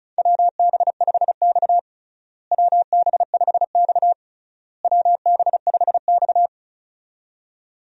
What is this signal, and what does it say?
Morse code, heard as a keyed tone, about 36 wpm, 700 Hz. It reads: WB5X WB5X WB5X
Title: USA Call Signs - 4 Characters - Repeated 3x - 35wpm